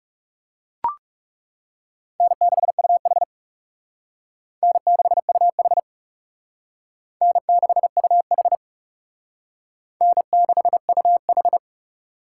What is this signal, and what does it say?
Morse code, heard as a keyed tone, about 35 wpm, 700 Hz. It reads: N6UH N6UH N6UH N6UH
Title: Speed Racing - USA 1x2 Call Signs 30wpm